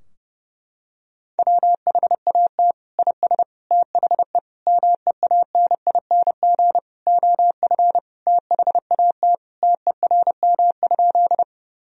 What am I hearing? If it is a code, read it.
Morse code, heard as a keyed tone, about 30 wpm, 700 Hz. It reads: WHAT IS THE MEANING OF THAT TERM?
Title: Sentences from Top 1000 Words - Review Part 4 - 30wpm